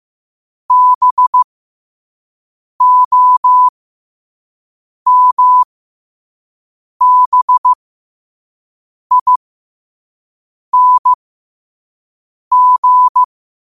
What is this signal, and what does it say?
Morse code, heard as a keyed tone, about 15 wpm, spaced as for 5 wpm, 1000 Hz. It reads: BOMBING
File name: qotdfiles / 5wpm